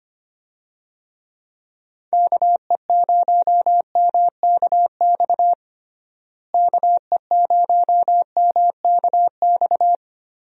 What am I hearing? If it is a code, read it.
Morse code, heard as a keyed tone, about 25 wpm, 700 Hz. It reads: KE0MKX KE0MKX
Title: USA Call Signs - 6 Characters - Repeated 3x - 25wpm